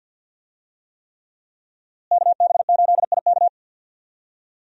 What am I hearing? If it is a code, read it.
Morse code, heard as a keyed tone, about 51 wpm, 700 Hz. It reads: KB8IK
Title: USA Call Signs - 5 Characters - Repeated 3x - 50wpm